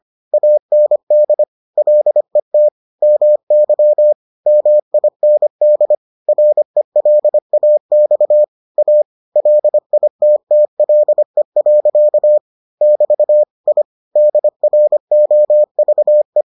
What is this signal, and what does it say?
Morse code, heard as a keyed tone, about 25 wpm, 600 Hz. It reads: AND LET MY MIND RELAX A LITTLE. <BT> I DROVE